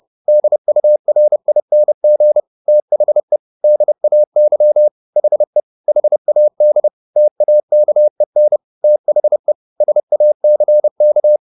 Morse code, heard as a keyed tone, about 30 wpm, 600 Hz. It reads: DURING THE DAY HE HAD TAKEN THE SACK
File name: the old man and the sea / the old man and the sea 1 30wpm